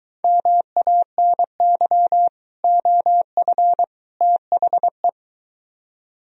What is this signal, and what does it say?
Morse code, heard as a keyed tone, about 23 wpm, 700 Hz. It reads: MANY OF THE